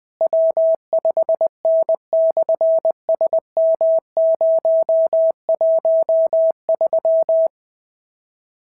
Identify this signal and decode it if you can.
Morse code, heard as a keyed tone, about 20 wpm, 690 Hz. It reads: W5N/SM013
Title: SOTA QSO Practice 20wpm